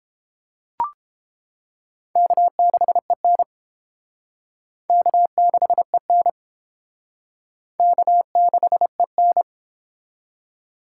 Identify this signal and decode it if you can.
Morse code, heard as a keyed tone, about 30 wpm, 700 Hz. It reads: K6EN K6EN K6EN